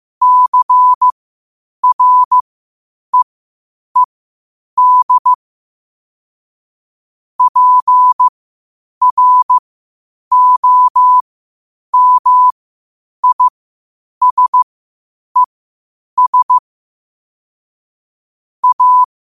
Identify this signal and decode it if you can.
Morse code, heard as a keyed tone, about 15 wpm, spaced as for 8 wpm, 1000 Hz. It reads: CREED PROMISES A
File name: qotdfiles / 8wpm